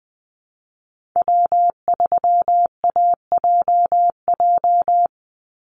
Morse code, heard as a keyed tone, about 20 wpm, 700 Hz. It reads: W3AJJ
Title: USA Call Signs 20wpm